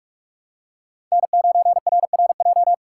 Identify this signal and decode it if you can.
Morse code, heard as a keyed tone, about 44 wpm, 700 Hz. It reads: N9RRJ
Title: USA Call Signs - 6x Character Spacing - 45wpm